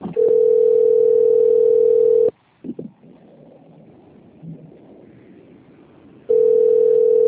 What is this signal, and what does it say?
Ringback tone (North America), 440 + 480 Hz.